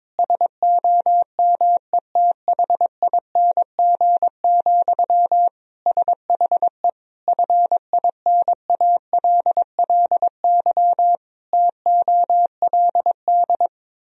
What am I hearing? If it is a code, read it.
Morse code, heard as a keyed tone, about 22 wpm, 700 Hz. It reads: SOMETHING, SHE FINALLY TOLD
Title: Sentences from Top 1000 Words - Review Part 3 - 22wpm